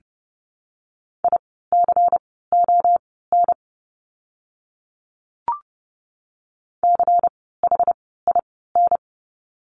Morse code, heard as a keyed tone, about 30 wpm, spaced as for 17 wpm, 700 Hz. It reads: ICON CHIN
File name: Lesson 16f - Letter c - Words 30wpm